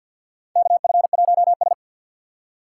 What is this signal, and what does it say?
Morse code, heard as a keyed tone, about 48 wpm, 700 Hz. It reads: KF1S
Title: USA Call Signs - 4 Characters - 6x Character Spacing - 50wpm